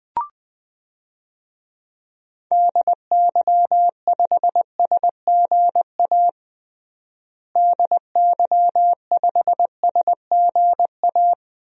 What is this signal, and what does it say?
Morse code, heard as a keyed tone, about 21 wpm, 700 Hz. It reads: DY5SGA DY5SGA
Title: DXCC-100 Call Signs - 6 Characters - Repeated 2x - 20wpm